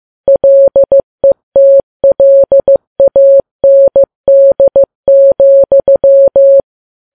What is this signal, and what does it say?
Morse code, heard as a keyed tone, about 15 wpm, 550 Hz. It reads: LETLAND,
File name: Files Web tls / Morse on9fv / morse15kps